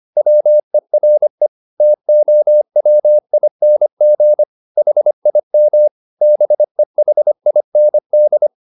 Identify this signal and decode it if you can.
Morse code, heard as a keyed tone, about 25 wpm, 600 Hz. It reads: WERE TOWING HIM BEHIND